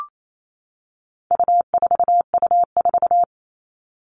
Morse code, heard as a keyed tone, about 28 wpm, 700 Hz. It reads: U4U4
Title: All Cut Numbers - 4-Digit 28wpm